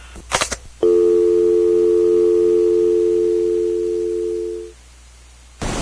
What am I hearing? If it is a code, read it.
Dial tone (North America), 350 + 440 Hz.